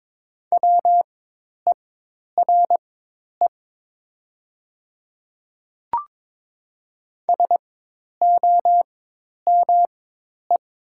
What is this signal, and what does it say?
Morse code, heard as a keyed tone, about 22 wpm, spaced as for 10 wpm, 700 Hz. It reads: WERE SOME
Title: 4x Character Spacing - Top 300 Words - 22wpm